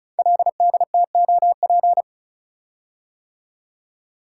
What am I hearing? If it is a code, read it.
Morse code, heard as a keyed tone, about 36 wpm, 690 Hz. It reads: LDTOP
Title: Code Groups - 5 Letters - 35wpm